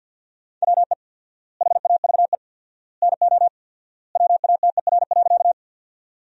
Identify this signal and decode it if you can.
Morse code, heard as a keyed tone, about 50 wpm, 700 Hz. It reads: WE HAVE NO WATER.
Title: 4x Word-Spacing - Sentences from Top 100 Words - 50wpm